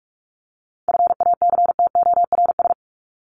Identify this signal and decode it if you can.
Morse code, heard as a keyed tone, about 44 wpm, 700 Hz. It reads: FACTORS
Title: ES:S Suffix - Top 100 Words with Base Words 45wpm